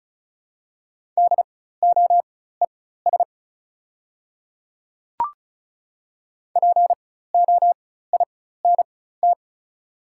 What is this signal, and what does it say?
Morse code, heard as a keyed tone, about 36 wpm, spaced as for 16 wpm, 700 Hz. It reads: DOES POINT
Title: 4x Character Spacing - Top 200 Words - 35wpm